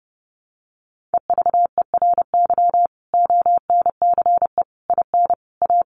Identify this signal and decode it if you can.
Morse code, heard as a keyed tone, about 28 wpm, 700 Hz. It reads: EVERY ONCE IN A
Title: Sets of 5 Words 30wpm